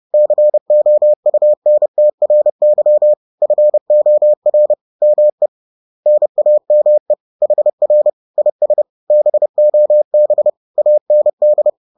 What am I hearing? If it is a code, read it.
Morse code, heard as a keyed tone, about 30 wpm, 600 Hz. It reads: COUNTRY FOR ME NAME HR IS BOB AND